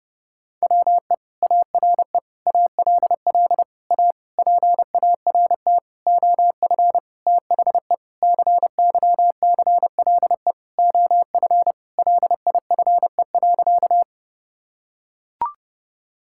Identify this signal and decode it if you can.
Morse code, heard as a keyed tone, about 30 wpm, 700 Hz. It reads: WE ARE ALL A PART OF THE CYCLE OF LIFE.
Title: Sentences from Top 1000 Words - Review Part 6 - 30wpm